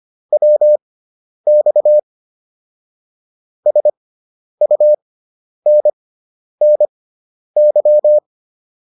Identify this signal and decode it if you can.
Morse code, heard as a keyed tone, about 25 wpm, spaced as for 10 wpm, 600 Hz. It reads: WX SUNNY
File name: QSO302 10